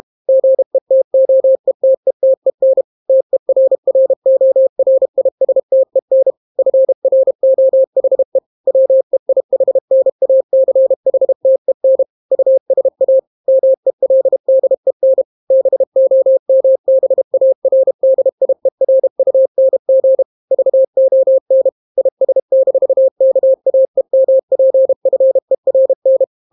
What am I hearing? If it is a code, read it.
Morse code, heard as a keyed tone, about 32 wpm, 530 Hz. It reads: GETOETETEN TERRORISTEN FROHE WEIHNACHTEN USA MELDEN BOMBARDIERUNG VON IS-KAEMPFERN